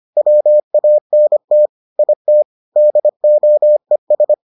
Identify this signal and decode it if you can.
Morse code, heard as a keyed tone, about 25 wpm, 600 Hz. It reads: WANT IT DOES